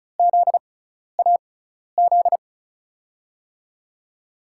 Morse code, heard as a keyed tone, about 36 wpm, spaced as for 12 wpm, 700 Hz. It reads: ZAZ
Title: USA Call Signs - 6x Character Spacing - 35wpm